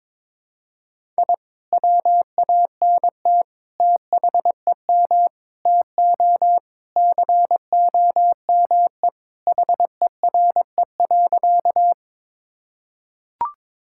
Morse code, heard as a keyed tone, about 22 wpm, 700 Hz. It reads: I WANT THEM TO COME HERE.
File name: practice / 22wpm / Sentences with 2-Word Phrases - 22wpm